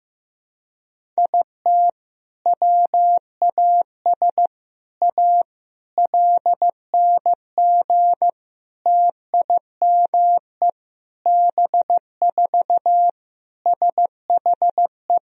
Morse code, heard as a keyed tone, about 16 wpm, 700 Hz. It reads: IT WAS A LNG TIME B4 SHE
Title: Sentences from Top 500 Words and Abbreviations 15wpm